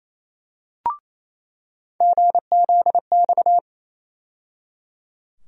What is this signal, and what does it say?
Morse code, heard as a keyed tone, about 28 wpm, 700 Hz. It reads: GZX
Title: USA Call Sign Postfixes - 3 Letters 28wpm